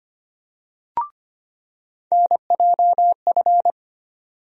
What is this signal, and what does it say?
Morse code, heard as a keyed tone, about 25 wpm, 700 Hz. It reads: NJF
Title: USA Call Sign Postfixes - 3 Letters 25wpm